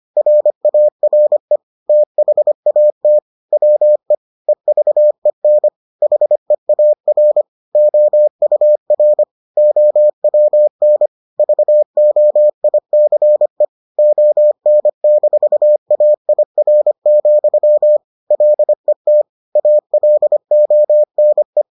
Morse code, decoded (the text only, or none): RARE THAT WE EVEN HEAR OUR OWN VOICE ON-AIR, LET ALONE